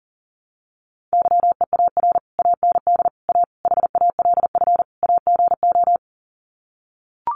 Morse code, heard as a keyed tone, about 40 wpm, 700 Hz. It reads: YEAR AND A HALF AGO
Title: Sets of 5 Words Encore 40wpm